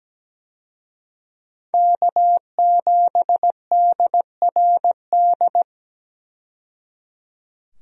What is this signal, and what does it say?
Morse code, heard as a keyed tone, about 17 wpm, 700 Hz. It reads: K7DRD
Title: USA Call Signs - 5 Characters - 17wpm